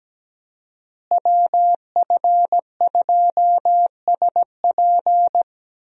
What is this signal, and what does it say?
Morse code, heard as a keyed tone, about 17 wpm, 700 Hz. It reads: WF2SP
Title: USA Call Signs - 5 Characters - Repeated 3x - 17wpm